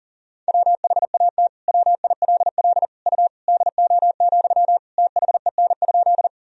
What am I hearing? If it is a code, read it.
Morse code, heard as a keyed tone, about 40 wpm, 700 Hz. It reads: WHAT WILL U DO, THEN?
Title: Sentences from Top 100 Words and Abbreviations 40wpm